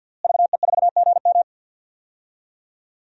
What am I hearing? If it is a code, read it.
Morse code, heard as a keyed tone, about 52 wpm, 700 Hz. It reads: VE4GM
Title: DXCC-100 Call Signs - 5 Characters - Repeated 3x - 50wpm